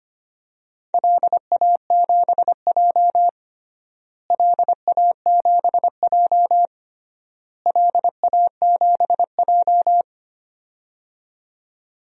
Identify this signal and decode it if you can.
Morse code, heard as a keyed tone, about 25 wpm, 700 Hz. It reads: LA7J LA7J LA7J
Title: DXCC-100 Call Signs - 4 Characters - Repeated 3x - 25wpm